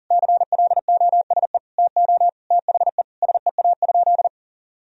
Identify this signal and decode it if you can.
Morse code, heard as a keyed tone, about 37 wpm, 700 Hz. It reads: CLOSE TO THE SEA?
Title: Sentences from Top 1000 Words - Review Part 1 - 40wpm